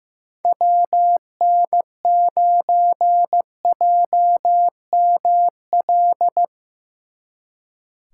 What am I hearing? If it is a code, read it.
Morse code, heard as a keyed tone, about 17 wpm, 700 Hz. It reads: WN9JML
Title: Speed Racing - USA 2x3 Call Signs 15wpm